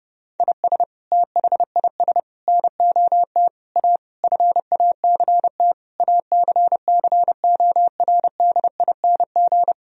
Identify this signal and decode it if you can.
Morse code, heard as a keyed tone, about 30 wpm, 700 Hz. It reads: IS THIS NOT A FACT ACCORDING
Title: Sentences from Top 1000 Words - Review Part 4 - 30wpm